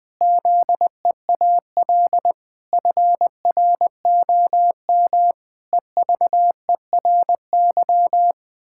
Morse code, heard as a keyed tone, about 21 wpm, 700 Hz. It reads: ZEAL FROM EVERY
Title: Pangram Challenge v2 - 20wpm